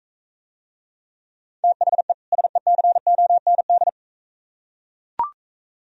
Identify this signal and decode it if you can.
Morse code, heard as a keyed tone, about 40 wpm, 700 Hz. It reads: THE SECOND